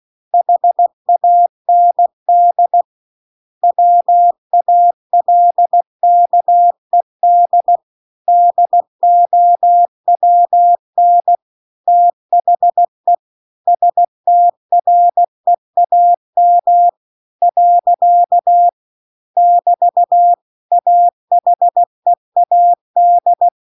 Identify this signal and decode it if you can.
Morse code, heard as a keyed tone, about 16 wpm, 700 Hz. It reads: HAND WALKED DOWN THE STREAM . <BT> AHEAD